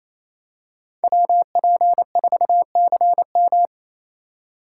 Morse code, heard as a keyed tone, about 28 wpm, 700 Hz. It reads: WP4CM
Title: USA Call Signs - 5 Characters - 4x Character Spacing - 28wpm